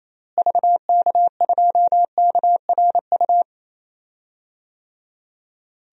Morse code, heard as a keyed tone, about 27 wpm, 700 Hz. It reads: VK2KRU
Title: DXCC-100 Call Signs - 6 Characters - Repeated 2x - 28wpm